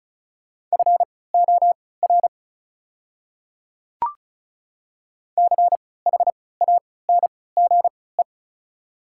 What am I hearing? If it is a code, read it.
Morse code, heard as a keyed tone, about 36 wpm, spaced as for 20 wpm, 700 Hz. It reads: FOR CHANGE
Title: 3x Character Spacing - Top 200 Words - 35wpm